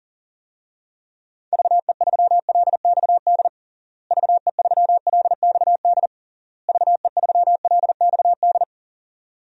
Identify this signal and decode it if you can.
Morse code, heard as a keyed tone, about 37 wpm, 700 Hz. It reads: VE3LXD VE3LXD VE3LXD
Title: DXCC-100 Call Signs - 6 Characters - Repeated 3x - 40wpm